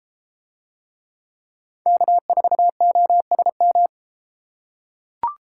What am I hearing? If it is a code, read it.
Morse code, heard as a keyed tone, about 31 wpm, 700 Hz. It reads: K4OSM